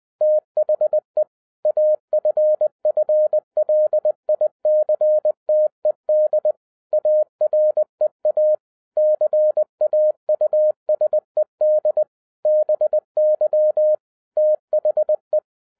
Morse code, heard as a keyed tone, about 20 wpm, 610 Hz. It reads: THE AFFLICTED AREA CAUSED BY THE
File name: 2011-10-mp3 / n03-TARO JR0QWW 20wpm